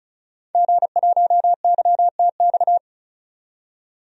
Morse code, heard as a keyed tone, about 36 wpm, 700 Hz. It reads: G1YTX
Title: DXCC-100 Call Signs - 5 Characters - 35wpm